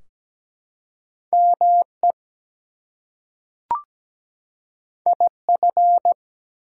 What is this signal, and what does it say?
Morse code, heard as a keyed tone, about 17 wpm, 700 Hz. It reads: ME IF
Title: Two Letter Words - 17wpm